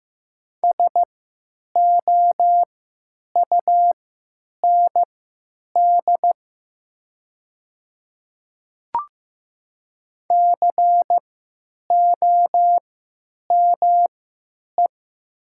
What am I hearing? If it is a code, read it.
Morse code, heard as a keyed tone, about 15 wpm, spaced as for 9 wpm, 700 Hz. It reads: SOUND COME
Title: 3x Character Spacing - Top 100 Words - 15wpm